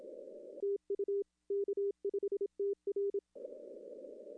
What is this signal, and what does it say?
Morse code, heard as a keyed tone, about 27 wpm, 410 Hz. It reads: TU K5TR
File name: contests / arrl-160-k5tr-op-w5tn -0300